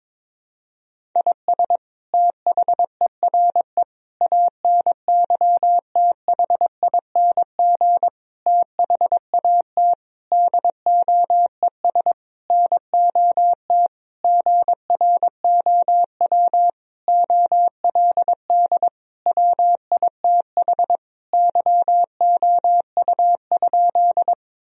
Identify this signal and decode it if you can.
Morse code, heard as a keyed tone, about 22 wpm, 700 Hz. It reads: IS THERE ANYTHING THAT DOES NOT GROW OLD WITH YOU?
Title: Sentences from Top 1000 Words - Review Part 1 - 22wpm